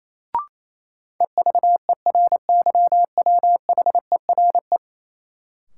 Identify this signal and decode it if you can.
Morse code, heard as a keyed tone, about 28 wpm, 700 Hz. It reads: EVERYWHERE
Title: Sentences from Top 1600 Words - New Word Drill - 28wpm